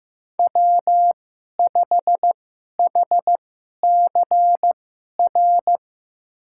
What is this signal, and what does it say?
Morse code, heard as a keyed tone, about 15 wpm, spaced as for 11 wpm, 700 Hz. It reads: W5HCR
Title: USA Call Signs - 5 Characters - 2x Character Spacing - 15wpm